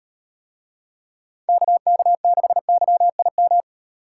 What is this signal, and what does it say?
Morse code, heard as a keyed tone, about 38 wpm, 700 Hz. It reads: KK6YIM